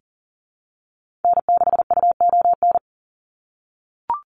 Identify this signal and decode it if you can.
Morse code, heard as a keyed tone, about 40 wpm, 690 Hz. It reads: N6UON
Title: USA Call Signs 40wpm